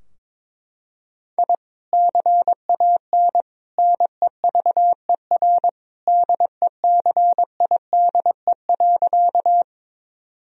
Morse code, heard as a keyed tone, about 23 wpm, 700 Hz. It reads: I CAN NEVER DECIDE.
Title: Sentences from Top 1000 Words - Review Part 3 - 22wpm